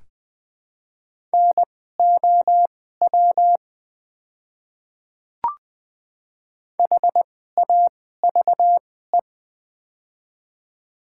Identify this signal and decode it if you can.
Morse code, heard as a keyed tone, about 21 wpm, spaced as for 15 wpm, 700 Hz. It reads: NOW HAVE
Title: 2x Character Spacing - Top 100 Words - 20wpm